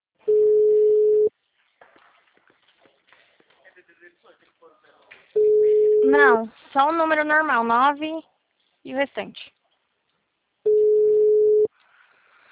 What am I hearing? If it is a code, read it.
Ringback tone (Europe, 425 Hz), with speech.